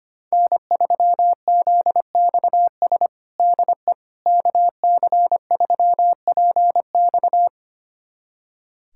Morse code, heard as a keyed tone, about 25 wpm, 700 Hz. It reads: N3ZXS DE KC3PX
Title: USA Callsign DE Callsign - 5 Characters - 25wpm